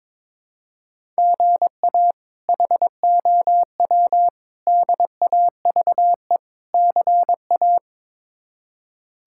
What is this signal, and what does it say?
Morse code, heard as a keyed tone, about 21 wpm, 700 Hz. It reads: GA HOW DAVE CA
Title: K1USN SST QSO Practice - 22wpm